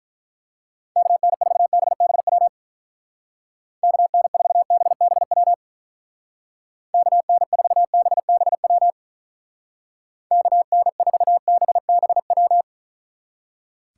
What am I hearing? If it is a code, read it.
Morse code, heard as a keyed tone, about 44 wpm, 700 Hz. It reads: KN4BBW KN4BBW KN4BBW KN4BBW